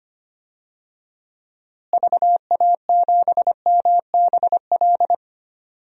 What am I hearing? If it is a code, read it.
Morse code, heard as a keyed tone, about 25 wpm, 700 Hz. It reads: VA7MBL
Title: DXCC-100 Call Signs - 6 Characters - Repeated 2x - 25wpm